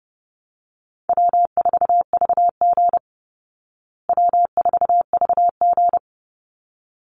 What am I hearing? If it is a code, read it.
Morse code, heard as a keyed tone, about 30 wpm, 700 Hz. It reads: W4VG W4VG
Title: Easy Call Signs Repeated 2x - 30wpm